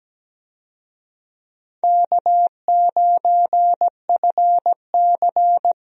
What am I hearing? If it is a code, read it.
Morse code, heard as a keyed tone, about 17 wpm, 700 Hz. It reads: K9FC
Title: USA Call Signs - 4 Characters - 17wpm